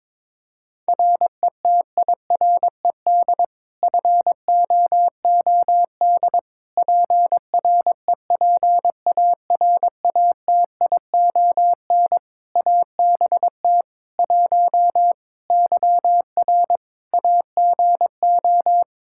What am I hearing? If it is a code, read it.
Morse code, heard as a keyed tone, about 22 wpm, 700 Hz. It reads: RETIRED FOOD PREPARATION ABT 1 YR AGO
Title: Practice QSO - Exchange 3 - More Info - 22wpm